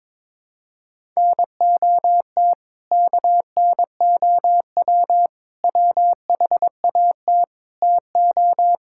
Morse code, heard as a keyed tone, about 22 wpm, 700 Hz. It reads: NOT KNOW WHAT TO
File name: practice / 22wpm / Build A Phrase - 5-Word Phrases - 22wpm